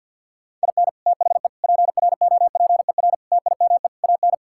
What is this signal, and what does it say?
Morse code, heard as a keyed tone, about 51 wpm, 700 Hz. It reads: IN THE PROPER TIME AN
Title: Sentences from Top 1000 Words - Review Part 5 - 50wpm